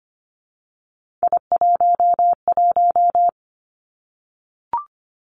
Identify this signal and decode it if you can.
Morse code, heard as a keyed tone, about 25 wpm, 700 Hz. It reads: I11
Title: Three Challenging Letter-Number Combinations 25wpm